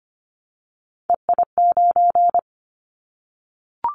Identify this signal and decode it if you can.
Morse code, heard as a keyed tone, about 25 wpm, 700 Hz. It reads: EI9
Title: Callsign Prefixes for Top 50 DXCC - 25wpm